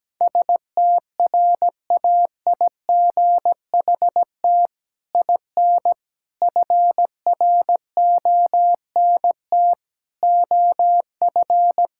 Morse code, heard as a keyed tone, about 17 wpm, 700 Hz. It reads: STRAIGHT IN FRONT OF
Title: Sentences from Top 1000 Words - Review Part 6 - 17wpm